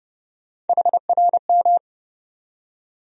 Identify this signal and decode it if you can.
Morse code, heard as a keyed tone, about 30 wpm, 700 Hz. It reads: HRM
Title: USA Call Sign Postfixes - 3 Letters 30wpm